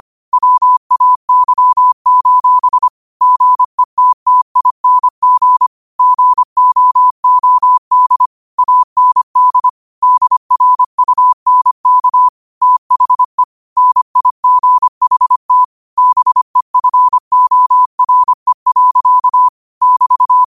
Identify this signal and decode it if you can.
Morse code, heard as a keyed tone, about 25 wpm, 1000 Hz. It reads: WAY: GETTING GOOD AND DRUNK THE NIGHT BEFORE. <BT>
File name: qotdfiles / 25wpm